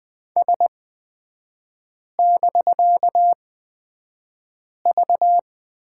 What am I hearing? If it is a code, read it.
Morse code, heard as a keyed tone, about 20 wpm, spaced as for 5 wpm, 700 Hz. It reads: S<BK>V
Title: 250 Lesson 40e - BK - Three Characters 20wpm